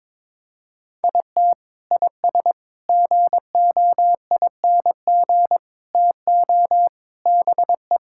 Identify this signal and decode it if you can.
Morse code, heard as a keyed tone, about 21 wpm, 700 Hz. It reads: IT IS GOING TO BE
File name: practice / 22wpm / Sentences with 4-Word Phrases - 22wpm